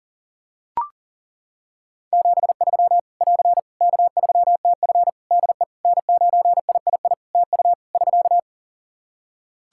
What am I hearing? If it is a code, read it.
Morse code, heard as a keyed tone, about 40 wpm, 700 Hz. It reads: 73 <AR> K3TF DE N9III TU <SK>
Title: Practice QSO - Exchange 4 - Ending - 40wpm